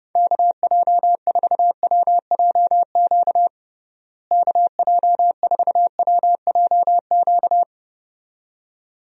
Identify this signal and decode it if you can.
Morse code, heard as a keyed tone, about 30 wpm, 700 Hz. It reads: KJ4WJQ KJ4WJQ
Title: USA Call Signs - 6 Characters - Repeated 2x - 30wpm